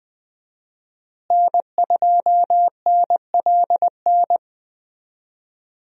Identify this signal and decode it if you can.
Morse code, heard as a keyed tone, about 20 wpm, 700 Hz. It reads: N2NLN
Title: USA Call Signs - 5 Characters - Repeated 2x - 20wpm